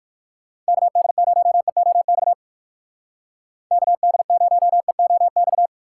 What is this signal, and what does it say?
Morse code, heard as a keyed tone, about 54 wpm, 700 Hz. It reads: KD0EOX KD0EOX
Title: Speed Racing - USA Call Signs 45wpm